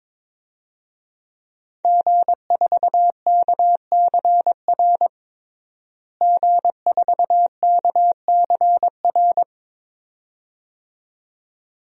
Morse code, heard as a keyed tone, about 22 wpm, 700 Hz. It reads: G4KCR G4KCR
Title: DXCC-100 Call Signs - 5 Characters - Repeated 2x - 22wpm